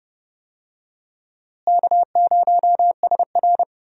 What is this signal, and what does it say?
Morse code, heard as a keyed tone, about 30 wpm, 700 Hz. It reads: K0SR
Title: USA Call Signs - 4 Characters - Repeated 2x - 30wpm